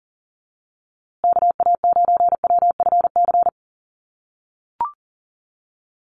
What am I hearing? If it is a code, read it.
Morse code, heard as a keyed tone, about 40 wpm, 700 Hz. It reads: KA9WFC
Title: USA Call Signs Repeated 2x 40wpm